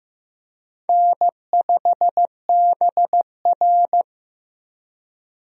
Morse code, heard as a keyed tone, about 15 wpm, 700 Hz. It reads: N5BR
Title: USA Call Signs - 4 Characters - Repeated 3x - 15wpm